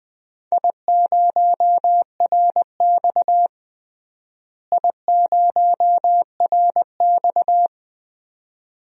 Morse code, heard as a keyed tone, about 20 wpm, 700 Hz. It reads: I0RX I0RX
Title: DXCC-100 Call Signs - 4 Characters - Repeated 2x - 20wpm